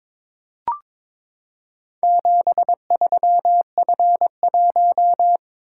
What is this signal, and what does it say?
Morse code, heard as a keyed tone, about 21 wpm, 700 Hz. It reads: 73F1
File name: practice / 22wpm / Four Letter-Number Combinations - 22wpm